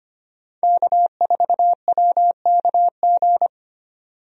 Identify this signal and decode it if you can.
Morse code, heard as a keyed tone, about 25 wpm, 700 Hz. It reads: K4WKG
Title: USA Call Signs - 5 Characters - 25wpm